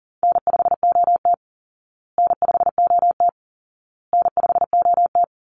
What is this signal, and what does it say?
Morse code, heard as a keyed tone, about 40 wpm, 700 Hz. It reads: N5OT N5OT N5OT
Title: Easy Call Signs Repeated 3x - 40wpm